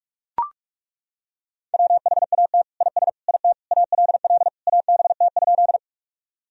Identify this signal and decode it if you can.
Morse code, heard as a keyed tone, about 42 wpm, 700 Hz. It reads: WHAT IS IT ALL ABT?
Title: Sentences from Top 100 Words and Abbreviations 45wpm